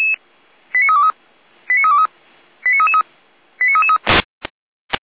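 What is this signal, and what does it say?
Five-tone selective call (ZVEI), heard as quick runs of five tones, tones of about 70 ms. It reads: 89323 89323 89333 89333
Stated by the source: Received by SDR.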